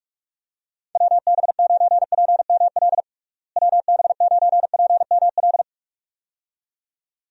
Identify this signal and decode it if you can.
Morse code, heard as a keyed tone, about 42 wpm, 700 Hz. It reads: WB9PML WB9PML
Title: USA Call Signs - 6 Characters - Repeated 2x - 45wpm